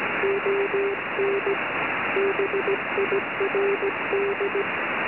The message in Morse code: ON BIRD